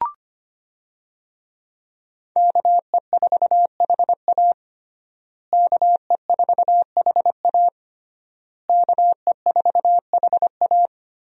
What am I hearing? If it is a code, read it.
Morse code, heard as a keyed tone, about 25 wpm, 700 Hz. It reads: KE4HA KE4HA KE4HA
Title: USA Call Signs - 5 Characters - Repeated 3x - 25wpm